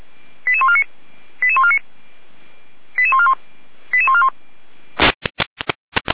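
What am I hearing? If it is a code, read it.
Five-tone selective call (ZVEI), heard as quick runs of five tones, tones of about 70 ms. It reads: 88149 88149 88141 88141
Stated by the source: Five-tone ZVEI selective call received on SDR, fire brigade.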